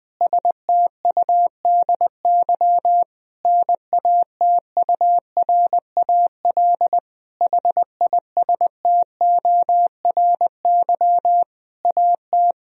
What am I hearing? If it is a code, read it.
Morse code, heard as a keyed tone, about 20 wpm, 700 Hz. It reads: STUDY NATURAL HISTORY AT
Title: Sentences from Top 1000 Words - Review Part 4 - 20wpm